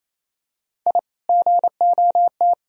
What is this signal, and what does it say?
Morse code, heard as a keyed tone, about 28 wpm, 700 Hz. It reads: I GOT
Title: Sentences from Top 1000 Words - Review Part 2 - 28wpm